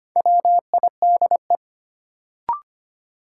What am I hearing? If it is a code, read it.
Morse code, heard as a keyed tone, about 25 wpm, 700 Hz. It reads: WIDE
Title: Sentences from Top 900 Words - New Word Drill - 25wpm